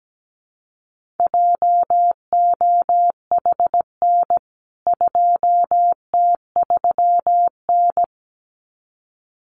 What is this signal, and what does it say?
Morse code, heard as a keyed tone, about 17 wpm, 700 Hz. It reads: JOHN 2T3N
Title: CWT Exchange 17wpm